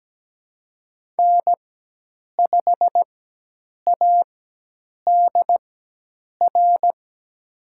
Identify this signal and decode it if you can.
Morse code, heard as a keyed tone, about 17 wpm, spaced as for 8 wpm, 700 Hz. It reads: N5ADR
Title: USA Call Signs - 5 Characters - 4x Character Spacing - 17wpm